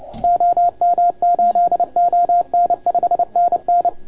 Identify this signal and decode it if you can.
Morse code, heard as a keyed tone, about 29 wpm, 690 Hz. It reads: OM8ON 5NN
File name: Archiv / CUC287 / OK1FGD CUC 287